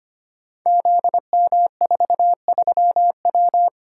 Morse code, heard as a keyed tone, about 25 wpm, 700 Hz. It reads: ZM43W
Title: Code Groups - 5 Letter:Numbers - 25wpm